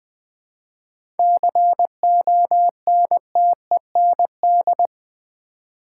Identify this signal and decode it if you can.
Morse code, heard as a keyed tone, about 21 wpm, 700 Hz. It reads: CONTEND
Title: 7 Letter Words - First 3 Letters Given - 20wpm